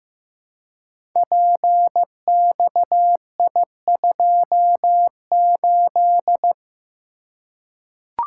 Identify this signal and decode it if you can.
Morse code, heard as a keyed tone, about 15 wpm, 700 Hz. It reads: PXI28
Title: Code Groups - 5 Letter:Numbers - 15wpm